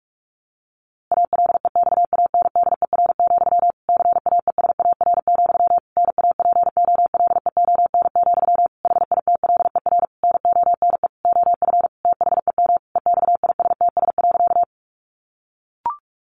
Morse code, heard as a keyed tone, about 44 wpm, 700 Hz. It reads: ALEXANDER, CAESAR, NAPOLEON, HITLER NONE OF THEM EXISTS.
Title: A Sound Of Thunder 45wpm